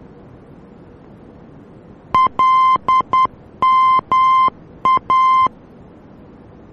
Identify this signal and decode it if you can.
Morse code, heard as a keyed tone, about 10 wpm, 1040 Hz. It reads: LMA